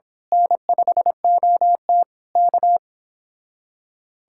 Morse code, heard as a keyed tone, about 25 wpm, 700 Hz. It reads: N5OT K
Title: Speed-Racing Common QSO Phrases - 22wpm